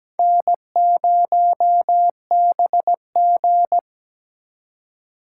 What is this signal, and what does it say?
Morse code, heard as a keyed tone, about 21 wpm, 700 Hz. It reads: N0BG
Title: Speed Racing - USA 1x2 Call Signs 17wpm